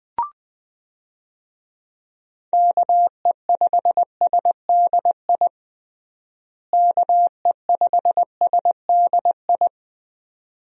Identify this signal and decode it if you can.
Morse code, heard as a keyed tone, about 21 wpm, 700 Hz. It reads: KE5SDI KE5SDI
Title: USA Call Signs - 6 Characters - Repeated 2x - 20wpm